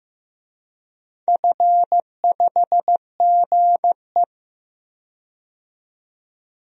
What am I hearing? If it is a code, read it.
Morse code, heard as a keyed tone, about 15 wpm, 700 Hz. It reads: F5GE
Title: DXCC-100 Call Signs - 4 Characters - 15wpm